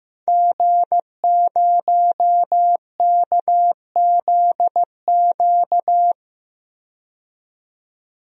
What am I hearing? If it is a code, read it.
Morse code, heard as a keyed tone, about 15 wpm, 700 Hz. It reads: G0KZQ
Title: DXCC-100 Call Signs - 5 Characters - 15wpm